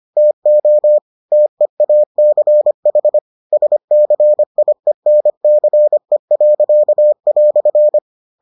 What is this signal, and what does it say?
Morse code, heard as a keyed tone, about 25 wpm, 600 Hz. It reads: TO TEACH SCIENCE."